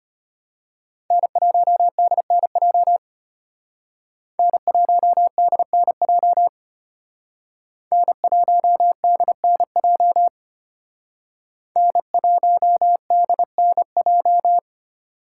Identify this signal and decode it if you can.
Morse code, heard as a keyed tone, about 29 wpm, 700 Hz. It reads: N1DNJ N1DNJ N1DNJ N1DNJ
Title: Speed Racing - USA Call Signs 25wpm